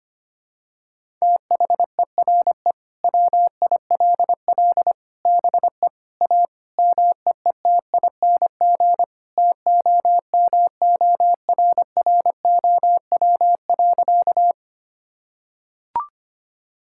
Morse code, heard as a keyed tone, about 25 wpm, 700 Hz. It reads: THERE WILL BE A MEETING TOMORROW.
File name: practice / 25wpm / Sentences with 4-Word Phrases - 25wpm